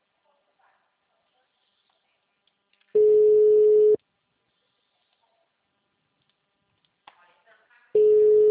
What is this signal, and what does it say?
Special information tone, with speech.